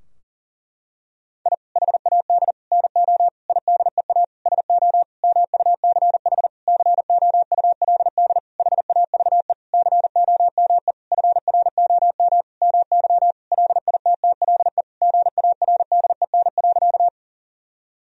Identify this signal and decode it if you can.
Morse code, heard as a keyed tone, about 37 wpm, 700 Hz. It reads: I HAD NO IDEA SO MUCH COULD HAVE COME FROM MY LITTLE GARDEN.
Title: Sentences from Top 1000 Words - Review Part 4 - 40wpm